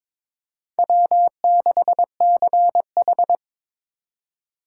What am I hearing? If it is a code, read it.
Morse code, heard as a keyed tone, about 22 wpm, 700 Hz. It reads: W6CH
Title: USA Call Signs - 4x Character Spacing - 22wpm